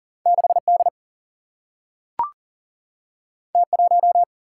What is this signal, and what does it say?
Morse code, heard as a keyed tone, about 40 wpm, 700 Hz. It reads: 6D T1
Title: Single-Letter Single-Number 40wpm